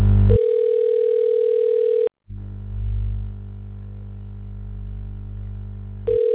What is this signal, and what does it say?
Ringback tone (North America), 440 + 480 Hz.